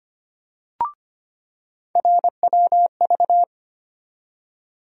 Morse code, heard as a keyed tone, about 25 wpm, 700 Hz. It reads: RWV
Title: USA Call Sign Postfixes - 3 Letters 25wpm